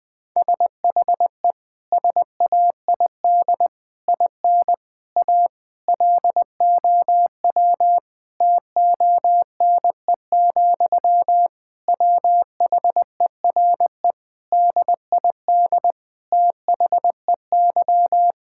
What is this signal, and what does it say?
Morse code, heard as a keyed tone, about 20 wpm, 700 Hz. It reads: SHE SAID IN A LOW TONE, WHERE DID THEY
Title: Sentences from Top 1600 Words 20wpm